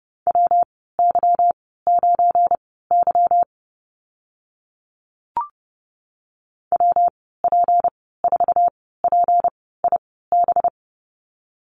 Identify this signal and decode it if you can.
Morse code, heard as a keyed tone, about 30 wpm, spaced as for 17 wpm, 700 Hz. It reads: WY9Y WP4PIB
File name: Lesson 37g - Letter x - Call Signs 30wpm